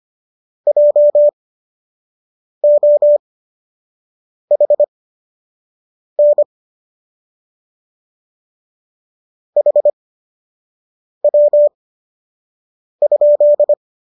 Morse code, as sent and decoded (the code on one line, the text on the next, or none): .--- --- .... -. / .... .-- ..--..
JOHN HW?